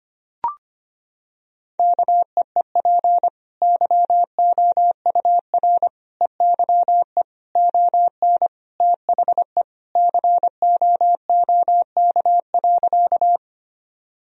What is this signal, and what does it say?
Morse code, heard as a keyed tone, about 25 wpm, 700 Hz. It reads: KEEP YOUR EYE ON THE COOK.
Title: Sentences from Top 1000 Words - Review Part 4 - 25wpm